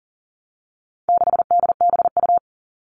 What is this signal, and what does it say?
Morse code, heard as a keyed tone, about 40 wpm, 700 Hz. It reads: 6DBU
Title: All Cut Numbers - 4-Digit 40wpm